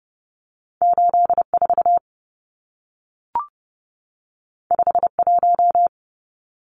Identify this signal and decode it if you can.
Morse code, heard as a keyed tone, about 30 wpm, 700 Hz. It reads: 84 51
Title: Numbers - 2-Digit 30wpm